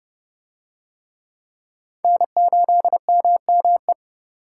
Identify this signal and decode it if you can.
Morse code, heard as a keyed tone, about 29 wpm, 700 Hz. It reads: N8MME